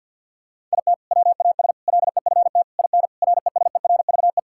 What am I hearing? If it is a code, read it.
Morse code, heard as a keyed tone, about 50 wpm, 700 Hz. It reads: IT WAS LEFT IN RESERVE
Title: Sentences from Top 1000 Words - Review Part 6 - 50wpm